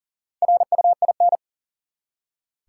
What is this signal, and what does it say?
Morse code, heard as a keyed tone, about 40 wpm, 700 Hz. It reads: RUIN
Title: Four Letter Words - 40wpm